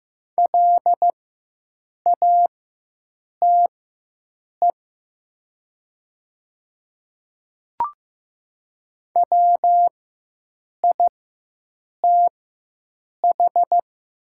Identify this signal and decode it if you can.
Morse code, heard as a keyed tone, about 15 wpm, spaced as for 7 wpm, 700 Hz. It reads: LATE WITH
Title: 4x Character Spacing - Top 300 Words - 15wpm